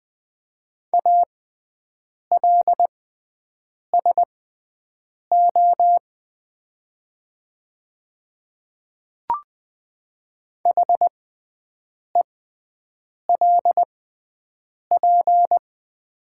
Morse code, heard as a keyed tone, about 20 wpm, spaced as for 7 wpm, 700 Hz. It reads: ALSO HELP
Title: 6x Character Spacing - Top 200 Words - 20wpm